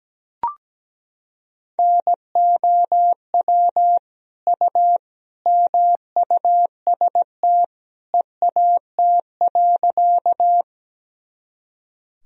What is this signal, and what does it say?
Morse code, heard as a keyed tone, about 17 wpm, 700 Hz. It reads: NOW U MUST EAT.
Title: Sentences from Top 300 Words and Abbreviations 17wpm